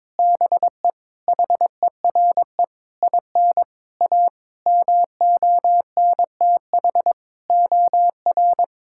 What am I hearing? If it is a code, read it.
Morse code, heard as a keyed tone, about 21 wpm, 700 Hz. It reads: BE HERE IN A MONTH OR
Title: Sentences from Top 1000 Words - Review Part 5 - 22wpm